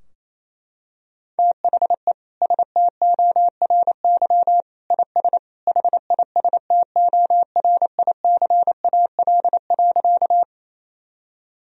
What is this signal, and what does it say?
Morse code, heard as a keyed tone, about 28 wpm, 700 Hz. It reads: THE STORY IS HISTORICAL.
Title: Sentences from Top 1000 Words - Review Part 4 - 28wpm